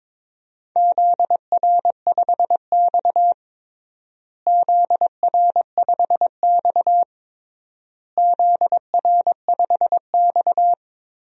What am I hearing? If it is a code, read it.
Morse code, heard as a keyed tone, about 22 wpm, 700 Hz. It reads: ZR5X ZR5X ZR5X
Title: DXCC-100 Call Signs - 4 Characters - Repeated 3x - 22wpm